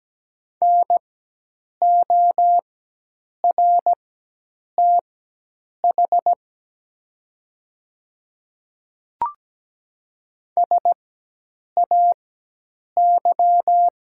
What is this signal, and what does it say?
Morse code, heard as a keyed tone, about 17 wpm, spaced as for 8 wpm, 700 Hz. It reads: NORTH SAY
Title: 4x Character Spacing - Top 300 Words - 17wpm